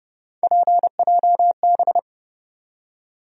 Morse code, heard as a keyed tone, about 30 wpm, 700 Hz. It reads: PJB
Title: USA Call Sign Postfixes - 3 Letters 30wpm